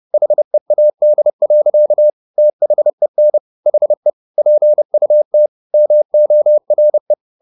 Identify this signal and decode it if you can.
Morse code, heard as a keyed tone, about 30 wpm, 600 Hz. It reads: HEAD. THEN HE PUT MORE